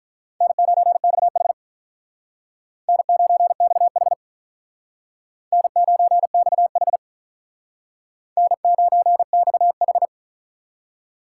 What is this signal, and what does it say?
Morse code, heard as a keyed tone, about 45 wpm, 700 Hz. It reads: N9XH N9XH N9XH N9XH